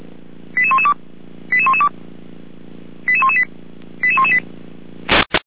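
Five-tone selective call (ZVEI), heard as quick runs of five tones, tones of about 70 ms. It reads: 88112 88112 88118 88118
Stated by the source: Received on an SDR.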